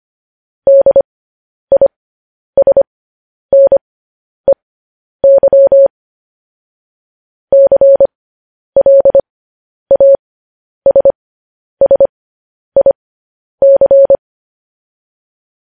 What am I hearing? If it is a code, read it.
Morse code, heard as a keyed tone, about 25 wpm, spaced as for 10 wpm, 550 Hz. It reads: DISNEY CLASSIC
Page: news-headlines-in-morse-code-at-10-wpm